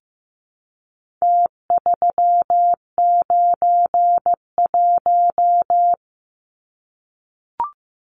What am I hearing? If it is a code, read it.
Morse code, heard as a keyed tone, about 15 wpm, 700 Hz. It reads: T391
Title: Numbers - Contest Style - 4-Digit 15wpm